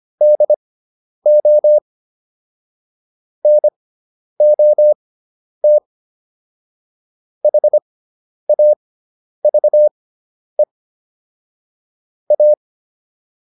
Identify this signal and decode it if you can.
Morse code, heard as a keyed tone, about 25 wpm, spaced as for 10 wpm, 600 Hz. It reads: DO NOT HAVE A
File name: WD401 10